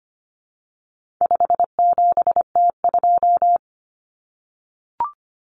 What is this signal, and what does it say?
Morse code, heard as a keyed tone, about 25 wpm, 700 Hz. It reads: 57T2
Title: Common Cut Numbers - 4-Digit 25wpm